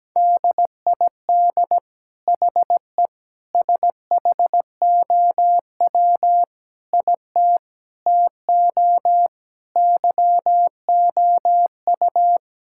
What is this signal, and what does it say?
Morse code, heard as a keyed tone, about 17 wpm, 700 Hz. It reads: DID HE SHOW IT TO YOU
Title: Sentences from Top 1600 Words 17wpm